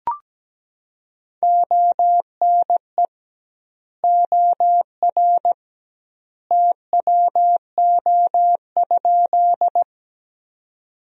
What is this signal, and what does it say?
Morse code, heard as a keyed tone, about 17 wpm, 700 Hz. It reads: ONE OR TWO?
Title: 2x Word-Spacing - Sentences from Top 100 Words - 17wpm